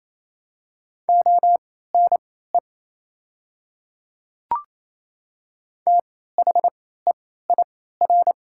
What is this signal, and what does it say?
Morse code, heard as a keyed tone, about 28 wpm, spaced as for 16 wpm, 700 Hz. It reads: ONE THEIR
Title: 3x Character Spacing - Top 100 Words - 28wpm